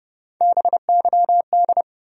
Morse code, heard as a keyed tone, about 30 wpm, 700 Hz. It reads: BYD
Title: USA Call Sign Postfixes - 3 Letters 30wpm